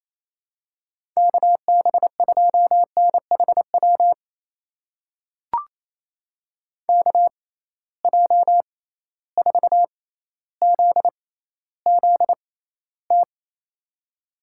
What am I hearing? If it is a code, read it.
Morse code, heard as a keyed tone, about 28 wpm, spaced as for 14 wpm, 700 Hz. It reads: KB2NHW KJ4ZZT
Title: USA Call Signs - 6 Characters - 6x Character Spacing - 28wpm